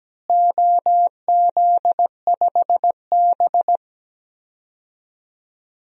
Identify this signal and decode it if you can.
Morse code, heard as a keyed tone, about 17 wpm, 700 Hz. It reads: OZ5B
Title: DXCC-100 Call Signs - 4 Characters - Repeated 3x - 17wpm